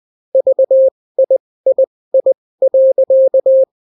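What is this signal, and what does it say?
Morse code, heard as a keyed tone, about 20 wpm, spaced as for 16 wpm, 530 Hz. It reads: VIII.
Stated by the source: Daily Morse weather and top stories at 16 wpm, clean without QRN.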